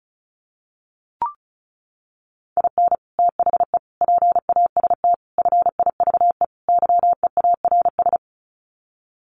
Morse code, heard as a keyed tone, about 34 wpm, 700 Hz. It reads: IN THE PAST FIVE YEARS
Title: Sets of 5 Words Encore 35wpm